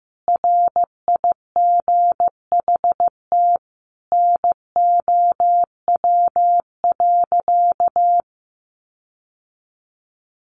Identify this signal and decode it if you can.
Morse code, heard as a keyed tone, about 15 wpm, 700 Hz. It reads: RIGHT NOW.
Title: Sentences from Top 1500 Words 15wpm